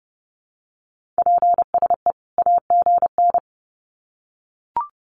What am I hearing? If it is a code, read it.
Morse code, heard as a keyed tone, about 30 wpm, 700 Hz. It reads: PSE AGN
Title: Common QSO Phrases 30wpm